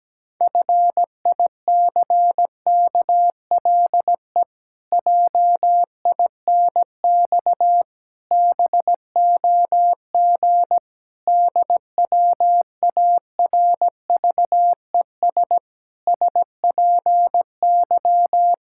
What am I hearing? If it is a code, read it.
Morse code, heard as a keyed tone, about 17 wpm, 700 Hz. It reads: FICKLE JINX BOG DWARVES SPY
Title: Pangram Challenge v2 - 17wpm